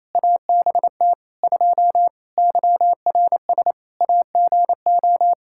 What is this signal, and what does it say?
Morse code, heard as a keyed tone, about 27 wpm, 700 Hz. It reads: ABT 2 YRS AGO
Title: Practice QSO - Exchange 3 - More Info - 28wpm